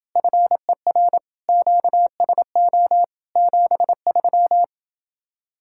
Morse code, heard as a keyed tone, about 30 wpm, 700 Hz. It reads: FER QSO 73